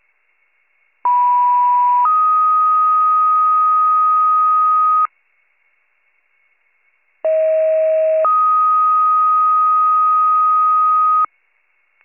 Two-tone page, heard as a one-second tone, then a three-second tone, about 980 then 1250 Hz, about 640 then 1250 Hz. It is heard from 1.0 to 11.3 s.